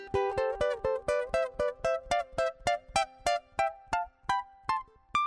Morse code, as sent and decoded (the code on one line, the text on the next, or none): none